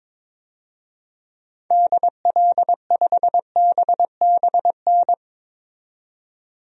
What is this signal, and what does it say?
Morse code, heard as a keyed tone, about 22 wpm, 700 Hz. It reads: DL5BBN
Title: DXCC-100 Call Signs - 6 Characters - 22wpm